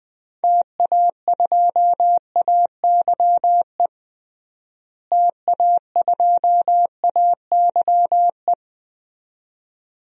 Morse code, heard as a keyed tone, about 20 wpm, 700 Hz. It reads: TA2AYE TA2AYE
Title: DXCC-100 Call Signs - 6 Characters - Repeated 2x - 20wpm